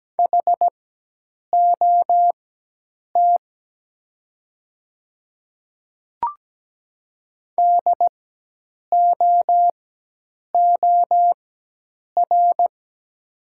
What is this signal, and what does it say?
Morse code, heard as a keyed tone, about 17 wpm, spaced as for 8 wpm, 700 Hz. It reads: HOT DOOR
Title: 4x Character Spacing - Top 300 Words - 17wpm